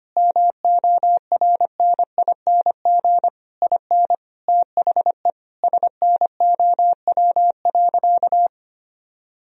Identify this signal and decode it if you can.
Morse code, heard as a keyed tone, about 25 wpm, 700 Hz. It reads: MORNING IN THE SNOW.
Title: Sentences from Top 1000 Words - Review Part 3 - 25wpm